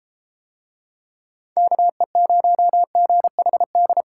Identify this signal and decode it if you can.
Morse code, heard as a keyed tone, about 34 wpm, 700 Hz. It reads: KE0GHD